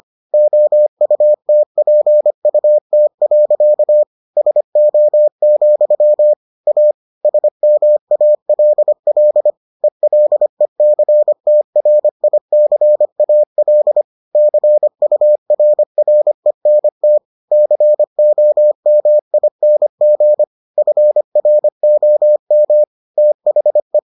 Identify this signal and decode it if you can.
Morse code, heard as a keyed tone, about 25 wpm, 600 Hz. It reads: OUTPUT. SO, A SMALL ELECTRICAL CURRENT COMING FROM THE